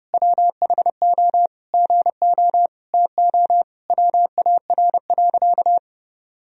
Morse code, heard as a keyed tone, about 30 wpm, 700 Hz. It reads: WHO GO TO WAR.
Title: Sentences from Top 1000 Words - Review Part 2 - 30wpm